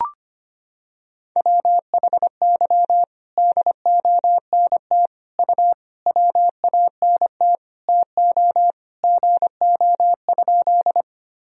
Morse code, heard as a keyed tone, about 25 wpm, 700 Hz. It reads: WHY DONT U WANT TO GO?
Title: Sentences from Top 300 Words and Abbreviations 25wpm